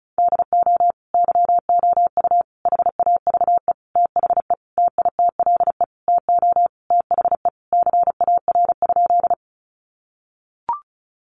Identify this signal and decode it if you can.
Morse code, heard as a keyed tone, about 34 wpm, 700 Hz. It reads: DO YOU HAVE THE TITLE TO THE CAR?
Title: Sentences from Top 1200 Words 35wpm